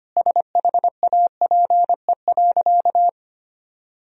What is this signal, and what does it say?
Morse code, heard as a keyed tone, about 25 wpm, 700 Hz. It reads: SHAPE.
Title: Sentences from Top 1000 Words - Review Part 3 - 25wpm